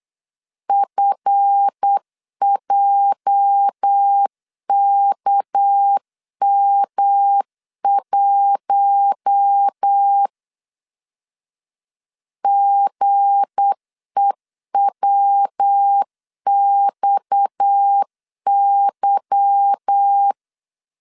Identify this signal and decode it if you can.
Morse code, heard as a keyed tone, about 8 wpm, 800 Hz. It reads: FJKM1 GEWXY
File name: LEZIONI / 8 YJ45 / 8 YJ 8wpm 3313